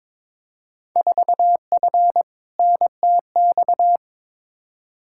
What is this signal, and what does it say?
Morse code, heard as a keyed tone, about 21 wpm, 700 Hz. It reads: 4F NTX
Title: Field Day Exchanges Encore - 22wpm